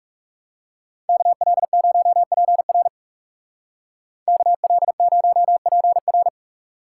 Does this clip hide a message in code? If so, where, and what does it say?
Yes: Morse code from 1.1 to 6.3 s, heard as a keyed tone, about 40 wpm, 700 Hz. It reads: KL0PR KL0PR